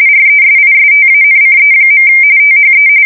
RTTY, about 45.45 baud. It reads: RESISTANCE. THE
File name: fsk170 / USS-Missouri